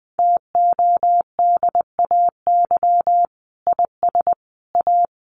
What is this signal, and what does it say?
Morse code, heard as a keyed tone, about 20 wpm, 700 Hz. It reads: TODAY IS A
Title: Sentences from Top 600 Words 20wpm